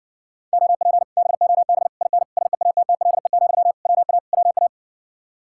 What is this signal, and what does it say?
Morse code, heard as a keyed tone, about 60 wpm, 700 Hz. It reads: OP BOB IN SEATTLE, WA WA